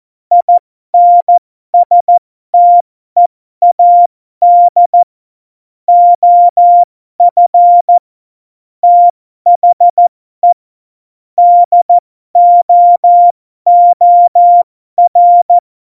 Morse code, heard as a keyed tone, about 14 wpm, spaced as for 12 wpm, 700 Hz. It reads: INSTEAD OF THE DOOR
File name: Morsekurs / Witze